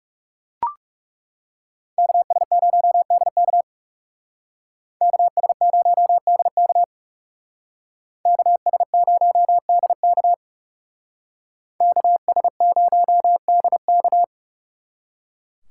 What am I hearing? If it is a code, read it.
Morse code, heard as a keyed tone, about 39 wpm, 700 Hz. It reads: KS0DK KS0DK KS0DK KS0DK